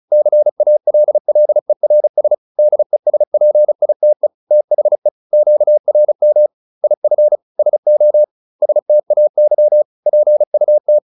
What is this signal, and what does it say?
Morse code, heard as a keyed tone, about 35 wpm, 600 Hz. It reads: CALLERS DESPITE THE QRM IF SO STAY PUT
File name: SS309 35